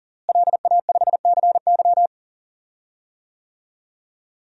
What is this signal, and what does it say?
Morse code, heard as a keyed tone, about 40 wpm, 700 Hz. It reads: LA5CY
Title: DXCC-100 Call Signs - 5 Characters - Repeated 3x - 40wpm